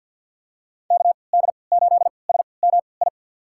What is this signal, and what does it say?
Morse code, heard as a keyed tone, about 48 wpm, spaced as for 28 wpm, 700 Hz. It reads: KD8SMI
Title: USA Call Signs - 3x Character Spacing - 50wpm